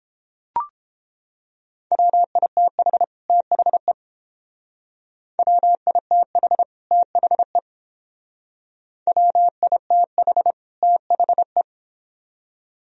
Morse code, heard as a keyed tone, about 30 wpm, 700 Hz. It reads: WITH THE WITH THE WITH THE